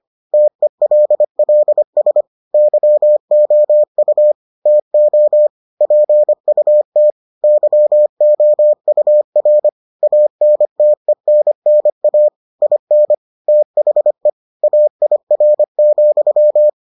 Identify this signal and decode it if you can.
Morse code, heard as a keyed tone, about 25 wpm, 600 Hz. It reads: TELLS YOU TO PUT YOUR ANTENNA IN THE AIR,